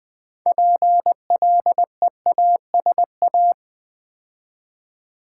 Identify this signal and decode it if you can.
Morse code, heard as a keyed tone, about 20 wpm, 700 Hz. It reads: PLEASA
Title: Build A Word - 3 Letters At A Time - 8-Letter Words - 20wpm